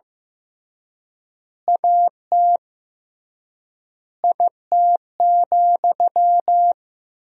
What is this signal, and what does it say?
Morse code, heard as a keyed tone, about 15 wpm, 700 Hz. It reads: AT IT,
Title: 3x Word-Spacing - Sentences from Top 200 Words - 15wpm